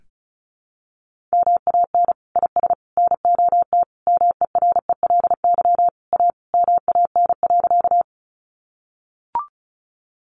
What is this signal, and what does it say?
Morse code, heard as a keyed tone, about 34 wpm, 700 Hz. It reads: MAN IS NOT MERELY A MAN.
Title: Sentences from Top 1400 Words 35wpm